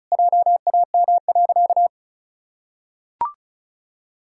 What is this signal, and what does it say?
Morse code, heard as a keyed tone, about 36 wpm, 700 Hz. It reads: JAM.
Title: Pangram Challenge v2 - 35wpm